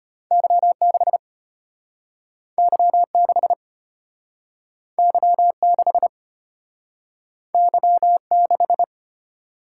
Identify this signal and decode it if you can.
Morse code, heard as a keyed tone, about 34 wpm, 700 Hz. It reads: Y6 Y6 Y6 Y6